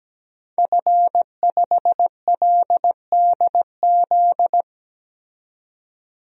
Morse code, heard as a keyed tone, about 17 wpm, 700 Hz. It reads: F5LDZ
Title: DXCC-100 Call Signs - 5 Characters - 17wpm